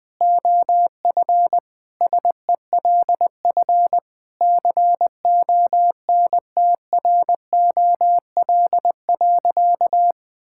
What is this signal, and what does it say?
Morse code, heard as a keyed tone, about 20 wpm, 700 Hz. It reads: OF SELF CONTROL.
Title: Sentences from Top 1000 Words - Review Part 5 - 20wpm